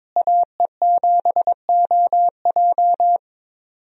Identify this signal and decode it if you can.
Morse code, heard as a keyed tone, about 22 wpm, 700 Hz. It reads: AE7OJ
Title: USA Call Signs - 5 Characters - Repeated 3x - 22wpm